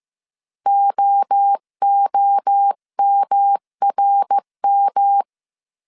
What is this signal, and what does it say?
Morse code, heard as a keyed tone, about 15 wpm, 800 Hz. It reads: OOMRM
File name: CW / DET / 1 3 1500C S A 14wpm 2447